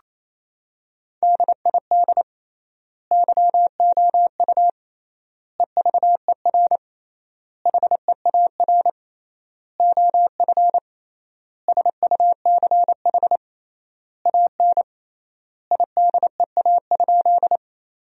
Morse code, heard as a keyed tone, about 28 wpm, 700 Hz. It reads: DID YOU EVER HEAR OF SUCH AN IDEA?
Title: 3x Word-Spacing - Sentences from Top 400 Words - 28wpm